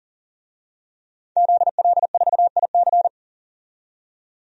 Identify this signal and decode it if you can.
Morse code, heard as a keyed tone, about 40 wpm, 700 Hz. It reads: ZL4IC
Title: DXCC-100 Call Signs - 5 Characters - 40wpm